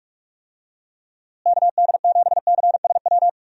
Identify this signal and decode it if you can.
Morse code, heard as a keyed tone, about 45 wpm, 700 Hz. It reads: KD7CSW
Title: USA Call Signs - 6 Characters - Repeated 3x - 45wpm